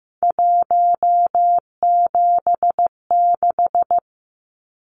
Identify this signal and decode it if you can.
Morse code, heard as a keyed tone, about 15 wpm, 700 Hz. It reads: 176
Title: Numbers - 3-Digit 15wpm